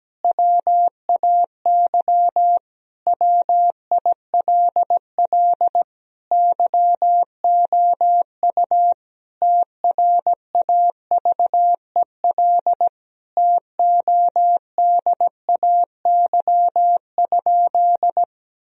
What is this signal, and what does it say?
Morse code, heard as a keyed tone, about 17 wpm, 700 Hz. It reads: WAY WILL YOU TRAVEL TODAY?
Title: Sentences from Top 1000 Words - Review Part 2 - 17wpm